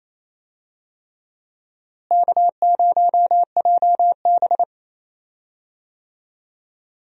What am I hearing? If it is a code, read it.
Morse code, heard as a keyed tone, about 28 wpm, 700 Hz. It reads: K0JB
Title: USA Call Signs - 4 Characters - Repeated 3x - 28wpm